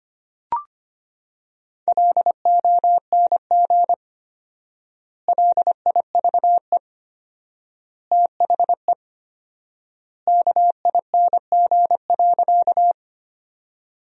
Morse code, heard as a keyed tone, about 25 wpm, 700 Hz. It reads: LONG LIVE THE KING.
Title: 4x Word-Spacing - Sentences from Top 400 Words - 25wpm